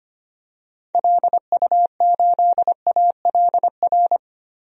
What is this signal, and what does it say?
Morse code, heard as a keyed tone, about 25 wpm, 700 Hz. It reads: LU8ALR
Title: DXCC-100 Call Signs - 6 Characters - 25wpm